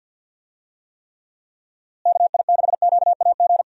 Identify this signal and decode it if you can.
Morse code, heard as a keyed tone, about 48 wpm, 700 Hz. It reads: KI6QAG
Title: USA Call Signs - 6 Characters - Repeated 3x - 50wpm